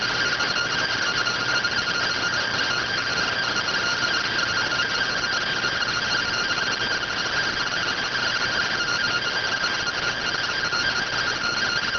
Two-tone teleprinter signal, about 170 Hz shift.